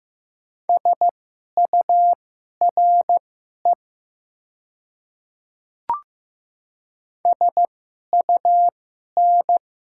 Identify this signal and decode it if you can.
Morse code, heard as a keyed tone, about 15 wpm, spaced as for 11 wpm, 700 Hz. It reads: SURE SUN
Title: 2x Character Spacing - Top 300 Words - 15wpm